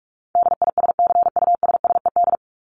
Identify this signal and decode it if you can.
Morse code, heard as a keyed tone, about 44 wpm, 700 Hz. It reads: DISCUSSED
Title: ED Suffix - Top 100 Words with Base Words 45wpm